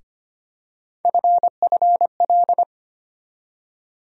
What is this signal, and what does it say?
Morse code, heard as a keyed tone, about 25 wpm, 700 Hz. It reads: FFL
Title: USA Call Sign Postfixes - 3 Letters 25wpm